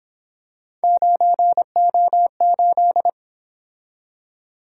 Morse code, heard as a keyed tone, about 26 wpm, 700 Hz. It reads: 9O8
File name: Speed-Racing - Three Letter-Number Combinations - 22wpm